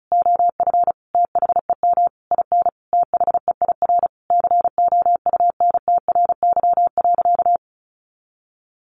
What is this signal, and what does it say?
Morse code, heard as a keyed tone, about 34 wpm, 700 Hz. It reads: OF THEM IN THEIR COUNTRY.
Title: Sentences from Top 1100 Words 35wpm